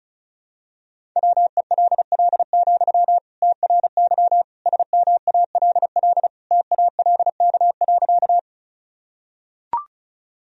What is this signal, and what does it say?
Morse code, heard as a keyed tone, about 36 wpm, 700 Hz. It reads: WELL, TRY SMALL TALK.
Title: Sentences from Top 1000 Words - Review Part 1 - 35wpm